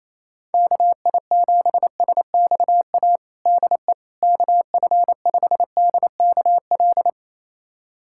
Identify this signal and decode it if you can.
Morse code, heard as a keyed tone, about 28 wpm, 700 Hz. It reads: KI7SXA DE KF5DKL
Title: USA Callsign DE Callsign - 6 Characters - 28wpm